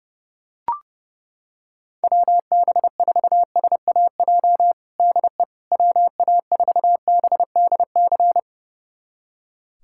Morse code, heard as a keyed tone, about 30 wpm, 700 Hz. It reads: WB4SAJ DE WA4BDC
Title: USA Callsign DE Callsign - 6 Characters - 30wpm